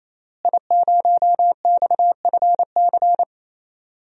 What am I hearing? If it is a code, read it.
Morse code, heard as a keyed tone, about 27 wpm, 700 Hz. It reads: I0XFC
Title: DXCC-100 Call Signs - 5 Characters - Repeated 3x - 28wpm